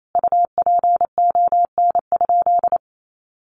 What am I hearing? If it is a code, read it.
Morse code, heard as a keyed tone, about 28 wpm, 700 Hz. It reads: UPON?
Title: Sentences from Top 1100 Words 28wpm